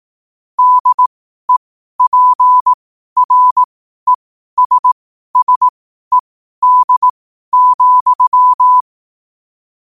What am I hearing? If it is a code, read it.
Morse code, heard as a keyed tone, about 18 wpm, spaced as for 12 wpm, 1000 Hz. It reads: DEPRESSED,
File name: qotdfiles / 12wpm